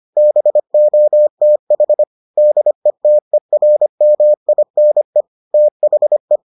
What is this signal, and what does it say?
Morse code, heard as a keyed tone, about 25 wpm, 600 Hz. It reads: BOTH DETERMINE THE